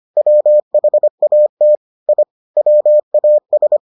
Morse code, heard as a keyed tone, about 25 wpm, 600 Hz. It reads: WHAT I WAS